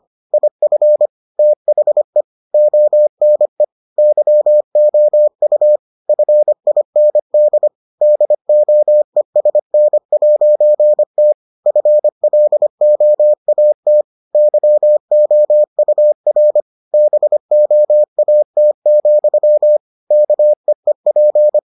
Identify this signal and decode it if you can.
Morse code, heard as a keyed tone, about 25 wpm, 600 Hz. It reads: IF THE ONE YOU FIND DOESN'T FLOAT YOUR BOAT, KEEP